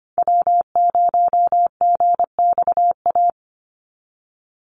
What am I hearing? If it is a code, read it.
Morse code, heard as a keyed tone, about 25 wpm, 700 Hz. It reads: W0GXA
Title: Call Signs Repeated 2x - 25wpm